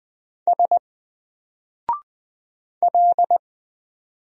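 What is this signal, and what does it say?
Morse code, heard as a keyed tone, about 20 wpm, 700 Hz. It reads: S L
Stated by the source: Single letter and number Morse drill at 20 wpm, from N4LCD.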